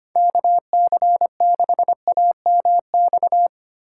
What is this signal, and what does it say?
Morse code, heard as a keyed tone, about 25 wpm, 690 Hz. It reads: KC6AMX
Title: USA Call Signs - 6 Characters - 2x Character Spacing - 25wpm